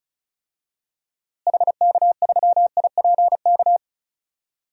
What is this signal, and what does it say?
Morse code, heard as a keyed tone, about 36 wpm, 700 Hz. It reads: HK3IPK
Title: DXCC-100 Call Signs - 6 Characters - Repeated 3x - 35wpm